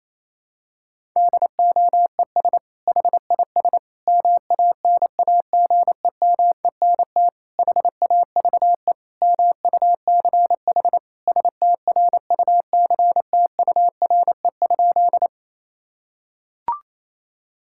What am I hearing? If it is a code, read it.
Morse code, heard as a keyed tone, about 28 wpm, 700 Hz. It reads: DOES HIS MANAGEMENT HAVE MUCH STRUCTURE?
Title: Sentences from Top 1000 Words - Review Part 6 - 28wpm